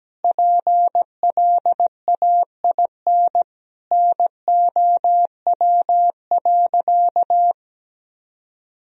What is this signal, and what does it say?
Morse code, heard as a keyed tone, about 17 wpm, 700 Hz. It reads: PLAIN NOW.
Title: Sentences from Top 1000 Words - Review Part 3 - 17wpm